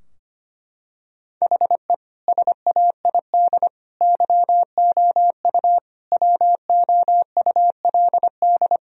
Morse code, heard as a keyed tone, about 25 wpm, 700 Hz. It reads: HE SAID YOU WOULD
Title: Sentences from Top 1000 Words - Review Part 2 - 25wpm